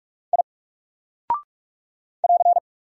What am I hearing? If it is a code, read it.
Morse code, heard as a keyed tone, about 44 wpm, 700 Hz. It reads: I <AR>
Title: Single Character (Letter:Number:Punctuation:Prosign) 45wpm